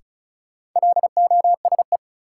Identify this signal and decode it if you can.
Morse code, heard as a keyed tone, about 36 wpm, 700 Hz. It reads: LOSE
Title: Sentences from Top 1100 Words - New Word Drill Alt - 35wpm